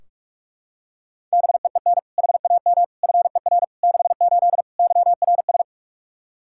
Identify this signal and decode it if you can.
Morse code, heard as a keyed tone, about 46 wpm, 700 Hz. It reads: BEEN HAM FER 68 YRS
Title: Practice QSO - Exchange 3 - More Info - 45wpm